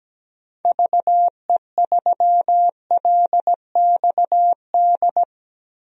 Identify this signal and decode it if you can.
Morse code, heard as a keyed tone, about 17 wpm, 700 Hz. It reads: VE3LXD
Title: DXCC-100 Call Signs - 6 Characters - Repeated 3x - 17wpm